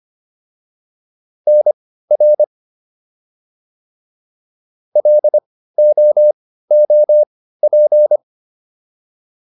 Morse code, heard as a keyed tone, about 25 wpm, spaced as for 15 wpm, 600 Hz. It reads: NR LOOP